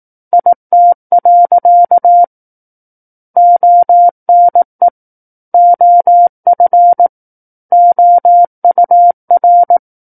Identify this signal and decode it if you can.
Morse code, heard as a keyed tone, about 18 wpm, 700 Hz. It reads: IT. ONE OF OUR
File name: Book One / 09-Chapter 9 - 18 WPM